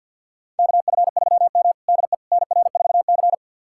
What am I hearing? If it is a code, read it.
Morse code, heard as a keyed tone, about 52 wpm, 700 Hz. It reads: KF3M DE NR4C
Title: USA Callsign DE Callsign - 4 Characters - 50wpm